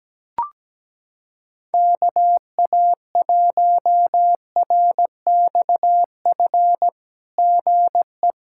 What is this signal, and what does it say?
Morse code, heard as a keyed tone, about 17 wpm, 700 Hz. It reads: KA1RXF GE
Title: SOTA QSO Practice 17wpm